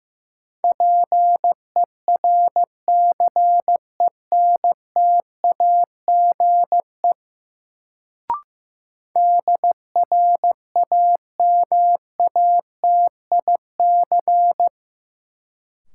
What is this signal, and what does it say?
Morse code, heard as a keyed tone, about 15 wpm, 700 Hz. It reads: PERCENTAGE DRAMATIC
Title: Sentences from Top 1300 Words - New Word Drill - 15wpm